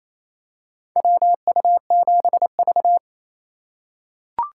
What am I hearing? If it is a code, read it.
Morse code, heard as a keyed tone, about 28 wpm, 700 Hz. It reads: WU7V
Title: USA Call Signs - 4 Characters - 3x Character Spacing - 28wpm